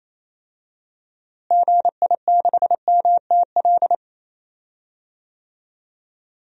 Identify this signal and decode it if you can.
Morse code, heard as a keyed tone, about 28 wpm, 700 Hz. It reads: GI6MTL
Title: DXCC-100 Call Signs - 6 Characters - Repeated 3x - 28wpm